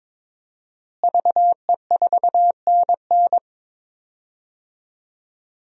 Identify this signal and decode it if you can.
Morse code, heard as a keyed tone, about 22 wpm, 700 Hz. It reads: VE4NN
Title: DXCC-100 Call Signs - 5 Characters - Repeated 2x - 22wpm